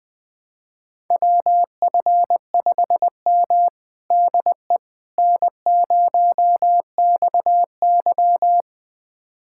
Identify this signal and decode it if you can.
Morse code, heard as a keyed tone, about 20 wpm, 700 Hz. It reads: WF5M DE N0XY
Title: USA Callsign DE Callsign - 4 Characters - 20wpm